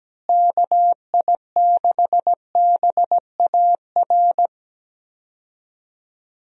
Morse code, heard as a keyed tone, about 17 wpm, 700 Hz. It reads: KI6BAR
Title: USA Call Signs - 6 Characters - Repeated 2x - 17wpm